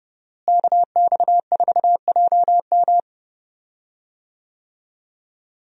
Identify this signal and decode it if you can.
Morse code, heard as a keyed tone, about 30 wpm, 700 Hz. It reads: KX4JM
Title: USA Call Signs - 5 Characters - Repeated 3x - 30wpm